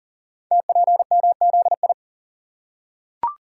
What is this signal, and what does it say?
Morse code, heard as a keyed tone, about 38 wpm, 700 Hz. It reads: TPMZI
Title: Code Groups - 5 Letters - 40wpm